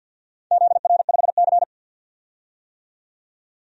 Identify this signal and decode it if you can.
Morse code, heard as a keyed tone, about 52 wpm, 700 Hz. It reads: ZR5C
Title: DXCC-100 Call Signs - 4 Characters - 50wpm